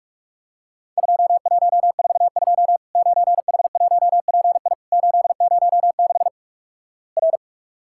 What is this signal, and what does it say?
Morse code, heard as a keyed tone, about 48 wpm, 700 Hz. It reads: 2142 9H1PI 806 R
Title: Worked All Europe DX Contest - QTC Exchange - 45wpm